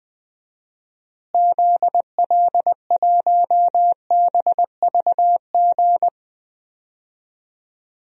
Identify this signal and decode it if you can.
Morse code, heard as a keyed tone, about 20 wpm, 700 Hz. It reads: ZL1BVG
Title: DXCC-100 Call Signs - 6 Characters - Repeated 3x - 20wpm